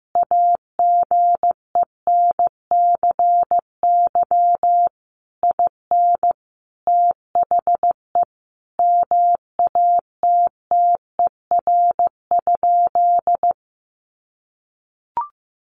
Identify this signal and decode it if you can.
Morse code, heard as a keyed tone, about 15 wpm, 700 Hz. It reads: AGENCY IN THE MATTER?
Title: Sentences from Top 1300 Words 15wpm